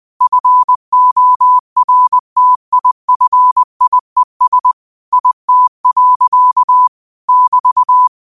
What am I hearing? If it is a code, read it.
Morse code, heard as a keyed tone, about 20 wpm, 1000 Hz. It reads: FORTIFIES IT. <BT>
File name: qotdfiles / 20wpm